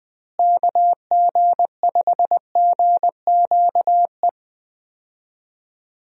Morse code, heard as a keyed tone, about 21 wpm, 700 Hz. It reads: KG5GQE
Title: USA Call Signs - 6 Characters - Repeated 3x - 20wpm